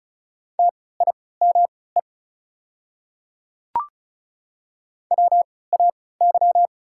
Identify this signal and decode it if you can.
Morse code, heard as a keyed tone, about 36 wpm, spaced as for 20 wpm, 690 Hz. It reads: TIME WAY
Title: 3x Character Spacing - Top 100 Words - 35wpm